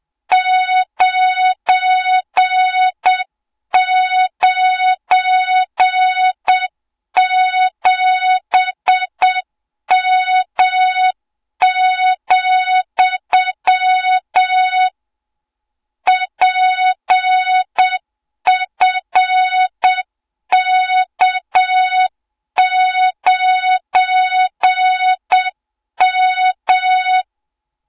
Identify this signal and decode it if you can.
Morse code, heard as a keyed tone, about 7 wpm, 3700 Hz, with noise. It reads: 997M, PFK9M
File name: Morze-HA2MN / hang / 20 GYAKORLAT